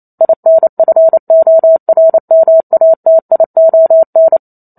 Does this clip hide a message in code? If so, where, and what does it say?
Yes: Morse code from 0.2 to 4.4 s, heard as a keyed tone, about 28 wpm, 650 Hz. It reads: INFORMATION